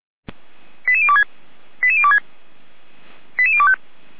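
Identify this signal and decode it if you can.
Five-tone selective call (ZVEI), heard as quick runs of five tones, tones of about 70 ms. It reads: 88026 88026 88025
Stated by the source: ZVEI five-tone call 88070 received on an SDR.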